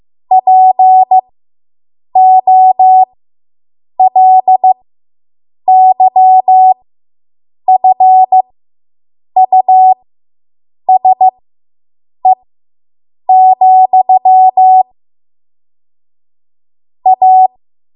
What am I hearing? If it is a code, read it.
Morse code, heard as a keyed tone, about 15 wpm, spaced as for 7 wpm, 750 Hz. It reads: POLYFUSE, A